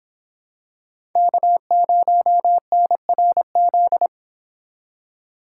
Morse code, heard as a keyed tone, about 27 wpm, 700 Hz. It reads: K0NRZ
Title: Speed Racing - USA 1x3 Call Signs 22wpm